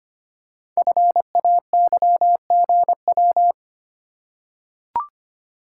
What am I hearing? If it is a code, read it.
Morse code, heard as a keyed tone, about 25 wpm, 700 Hz. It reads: FAYGW
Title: Code Groups - 5 Letters - 25wpm